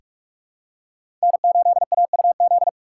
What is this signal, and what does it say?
Morse code, heard as a keyed tone, about 44 wpm, 690 Hz. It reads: N8AUZ
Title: USA Call Signs - 5 Characters - 6x Character Spacing - 45wpm